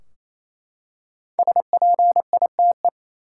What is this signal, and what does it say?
Morse code, heard as a keyed tone, about 28 wpm, 700 Hz. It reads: SPITE
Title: Five Letter Words - 28wpm